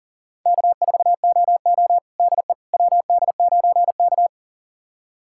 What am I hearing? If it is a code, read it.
Morse code, heard as a keyed tone, about 40 wpm, 700 Hz. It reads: K4OO DE WD9K
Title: USA Callsign DE Callsign - 4 Characters - 40wpm